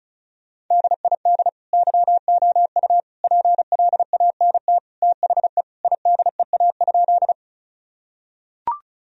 Morse code, heard as a keyed tone, about 34 wpm, 700 Hz. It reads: DID YOU PLANT THE IDEA?
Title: Sentences from Top 1000 Words - Review Part 1 - 35wpm